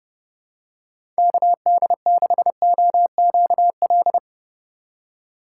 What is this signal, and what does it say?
Morse code, heard as a keyed tone, about 30 wpm, 700 Hz. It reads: KD6OQL
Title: USA Call Signs - 6 Characters - 6x Character Spacing - 30wpm